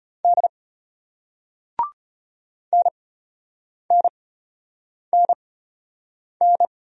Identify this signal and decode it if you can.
Morse code, heard as a keyed tone, about 33 wpm, spaced as for 8 wpm, 690 Hz. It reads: D NNNN